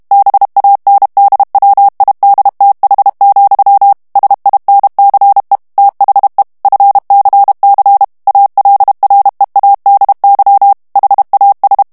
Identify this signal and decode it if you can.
Morse code, heard as a keyed tone, about 32 wpm, 800 Hz. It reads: BANDWIDTH, SINCE THE FCC ALREADY HAS
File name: morse mp3 / ANGLAIS texte clair 3 vitesse 30 mots minute